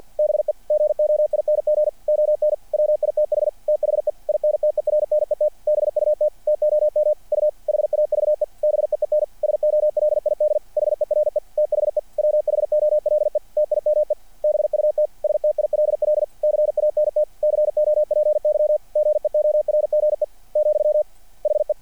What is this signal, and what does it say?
Morse code, heard as a keyed tone, about 50 wpm, 600 Hz. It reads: BE GOING ON WITH THE INTERNET BUT TOM U HAVE BEEN SOLID HERE THE WHOLE TIME BUT STILL CANT COPY GEORGE, HE
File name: archived / CRSnet